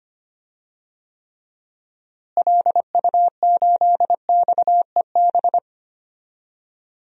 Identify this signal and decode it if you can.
Morse code, heard as a keyed tone, about 25 wpm, 700 Hz. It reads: LU8XEB
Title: DXCC-100 Call Signs - 6 Characters - Repeated 2x - 25wpm